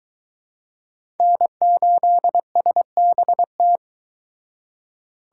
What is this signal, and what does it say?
Morse code, heard as a keyed tone, about 22 wpm, 700 Hz. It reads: N8SBT